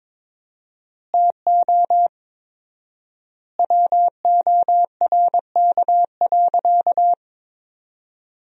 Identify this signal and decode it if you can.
Morse code, heard as a keyed tone, about 21 wpm, 700 Hz. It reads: TO WORK.
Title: 4x Word-Spacing - Sentences from Top 300 Words - 22wpm